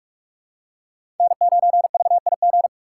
Morse code, heard as a keyed tone, about 46 wpm, 700 Hz. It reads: N9VIG
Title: USA Call Signs - 4x Character Spacing - 45wpm